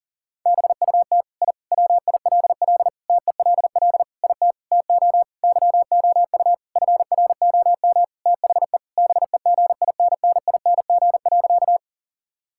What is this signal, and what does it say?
Morse code, heard as a keyed tone, about 40 wpm, 700 Hz. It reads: BUT I WILL TELL IT TO YOU FROM THE BEGINNING.
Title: Sentences from Top 1000 Words - Review Part 5 - 40wpm